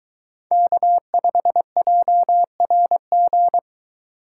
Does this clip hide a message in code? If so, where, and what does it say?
Yes: Morse code from 0.5 to 3.6 s, heard as a keyed tone, about 23 wpm, 700 Hz. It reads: K5JRG